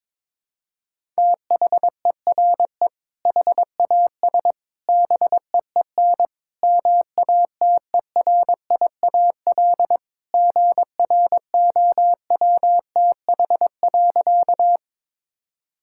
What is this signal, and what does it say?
Morse code, heard as a keyed tone, about 22 wpm, 700 Hz. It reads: THERE HAS BEEN MATERIAL GROWTH.
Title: Sentences from Top 1000 Words - Review Part 5 - 22wpm